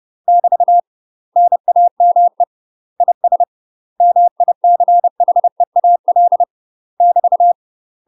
Morse code, heard as a keyed tone, about 30 wpm, 700 Hz. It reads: <BT> NAME IS MICHEAL <BT>
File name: UK MASTER QSOs TRAINING / 30WPM / UK MASTER QSOs 0015